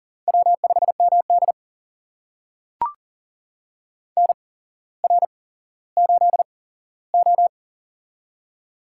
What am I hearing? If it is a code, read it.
Morse code, heard as a keyed tone, about 43 wpm, spaced as for 17 wpm, 700 Hz. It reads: W5MD NR8O
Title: USA Call Signs - 4 Characters - 8x Character Spacing - 40wpm